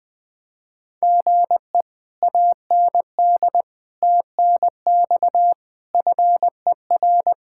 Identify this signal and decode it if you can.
Morse code, heard as a keyed tone, about 20 wpm, 700 Hz. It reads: GE AND TNX FER
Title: Speed-Racing Common QSO Phrases - 20wpm